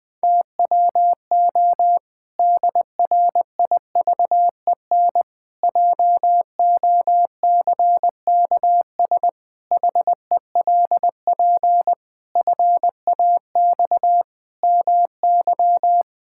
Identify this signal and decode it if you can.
Morse code, heard as a keyed tone, about 20 wpm, 700 Hz. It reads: TWO DRIVEN JOCKS HELP FAX MY
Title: Pangram Challenge v2 - 20wpm